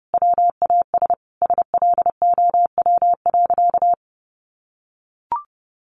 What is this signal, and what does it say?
Morse code, heard as a keyed tone, about 30 wpm, 700 Hz. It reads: WAS SLOW.
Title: Sentences from Top 1400 Words 30wpm